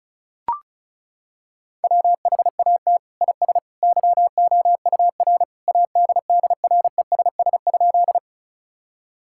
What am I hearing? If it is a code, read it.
Morse code, heard as a keyed tone, about 36 wpm, 700 Hz. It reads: WHAT IS YOUR ADDRESS?
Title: Sentences from Top 1000 Words - Review Part 6 - 35wpm